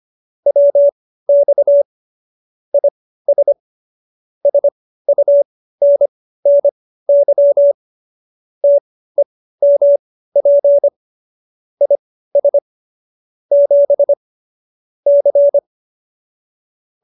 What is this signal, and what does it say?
Morse code, heard as a keyed tone, about 25 wpm, spaced as for 15 wpm, 570 Hz. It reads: WX IS SUNNY TEMP IS 7 C